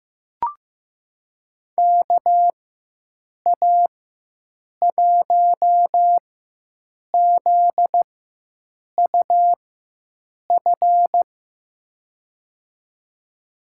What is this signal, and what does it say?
Morse code, heard as a keyed tone, about 15 wpm, spaced as for 7 wpm, 700 Hz. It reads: KA1ZUF
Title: USA Call Signs - 6 Characters - 4x Character Spacing - 15wpm